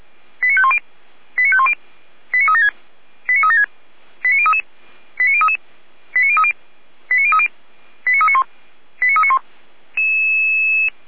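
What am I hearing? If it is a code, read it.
Five-tone selective call (ZVEI), heard as quick runs of five tones, tones of about 70 ms. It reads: 89510 89510 89376 89376 89033 89033 89030 89030 89391 89391
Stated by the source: ZVEI five-tone call 89050 received by SDR.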